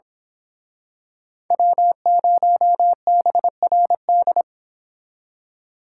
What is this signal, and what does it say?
Morse code, heard as a keyed tone, about 26 wpm, 700 Hz. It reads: W0BRD
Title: Speed Racing - USA 1x3 Call Signs 22wpm